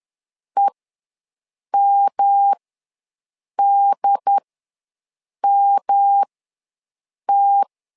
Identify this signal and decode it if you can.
Morse code, heard as a keyed tone, about 10 wpm, spaced as for 6 wpm, 800 Hz. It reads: EMDMT
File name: LEZIONI / 3 MWRO / 3 M ws 10wpm 2121